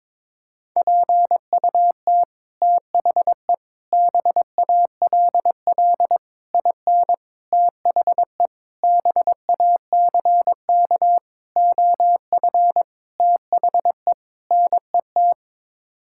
Morse code, heard as a keyed tone, about 22 wpm, 700 Hz. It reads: PUT THE BALL IN THE BACK OF THE NET
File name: Sets of 9 Words - 22wpm